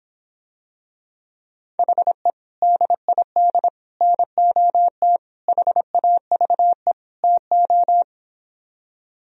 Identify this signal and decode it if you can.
Morse code, heard as a keyed tone, about 26 wpm, 700 Hz. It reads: HE DID NOT HAVE TO